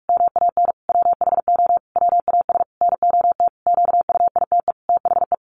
Morse code, heard as a keyed tone, about 44 wpm, 700 Hz. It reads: MAN WHO WAS NOT QUITE THE
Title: A Sound Of Thunder 45wpm